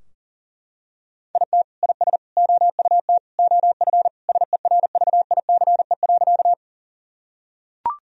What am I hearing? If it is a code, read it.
Morse code, heard as a keyed tone, about 40 wpm, 700 Hz. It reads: IT IS OUT OF SERVICE.
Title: Sentences from Top 1000 Words - Review Part 4 - 40wpm